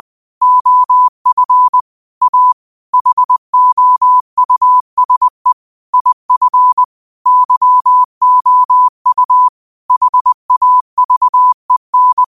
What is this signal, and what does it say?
Morse code, heard as a keyed tone, about 20 wpm, 1000 Hz. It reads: OF A HOUSE IF YOU HAVEN
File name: qotdfiles / 20wpm